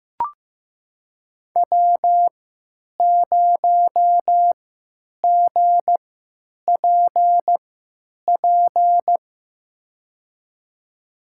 Morse code, heard as a keyed tone, about 15 wpm, spaced as for 9 wpm, 700 Hz. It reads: W0GPP
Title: USA Call Signs - 5 Characters - 3x Character Spacing - 15wpm